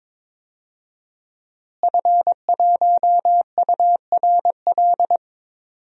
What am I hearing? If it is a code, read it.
Morse code, heard as a keyed tone, about 22 wpm, 700 Hz. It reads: F1URL
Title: DXCC-100 Call Signs - 5 Characters - Repeated 3x - 22wpm